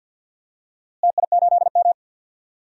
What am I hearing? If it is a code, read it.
Morse code, heard as a keyed tone, about 46 wpm, 700 Hz. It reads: TI8M
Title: DXCC-100 Call Signs - 4 Characters - Repeated 2x - 50wpm